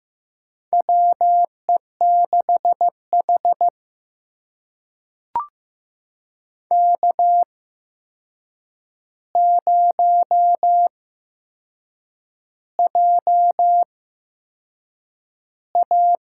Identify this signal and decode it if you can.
Morse code, heard as a keyed tone, about 15 wpm, spaced as for 6 wpm, 700 Hz. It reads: WE6H K0JA
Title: USA Call Signs - 4 Characters - 8x Character Spacing - 15wpm